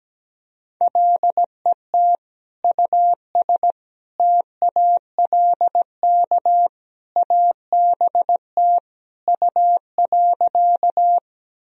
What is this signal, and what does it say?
Morse code, heard as a keyed tone, about 17 wpm, 700 Hz. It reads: LET US TALK ABT U.
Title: Sentences from Top 400 Words and Abbreviations 17wpm